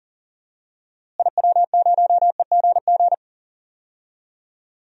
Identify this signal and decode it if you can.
Morse code, heard as a keyed tone, about 40 wpm, 700 Hz. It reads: IW0EGG
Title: DXCC-100 Call Signs - 6 Characters - Repeated 2x - 40wpm